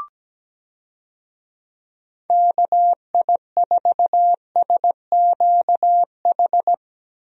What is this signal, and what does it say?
Morse code, heard as a keyed tone, about 17 wpm, 700 Hz. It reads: KI4SQH
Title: USA Call Signs - 6 Characters - Repeated 3x - 17wpm